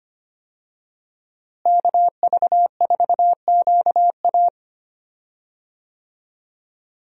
Morse code, heard as a keyed tone, about 25 wpm, 700 Hz. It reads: KV4QA
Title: USA Call Signs - 5 Characters - Repeated 3x - 25wpm